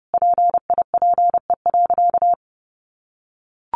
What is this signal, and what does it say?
Morse code, heard as a keyed tone, about 30 wpm, 700 Hz. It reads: PIPE.
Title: Sentences from Top 1500 Words 30wpm